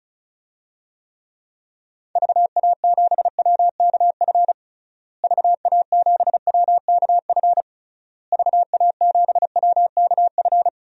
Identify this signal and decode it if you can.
Morse code, heard as a keyed tone, about 36 wpm, 700 Hz. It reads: VA7WKF VA7WKF VA7WKF
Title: DXCC-100 Call Signs - 6 Characters - Repeated 3x - 35wpm